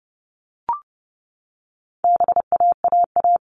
Morse code, heard as a keyed tone, about 30 wpm, 700 Hz. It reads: BAAA
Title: All Cut Numbers - 4-Digit 30wpm